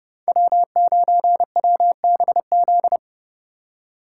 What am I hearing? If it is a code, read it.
Morse code, heard as a keyed tone, about 37 wpm, 700 Hz. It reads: W9WBZ
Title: Speed Racing - USA 1x3 Call Signs 30wpm